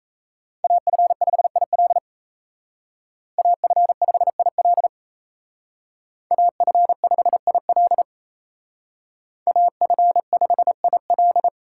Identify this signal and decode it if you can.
Morse code, heard as a keyed tone, about 35 wpm, 700 Hz. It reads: AF5IL AF5IL AF5IL AF5IL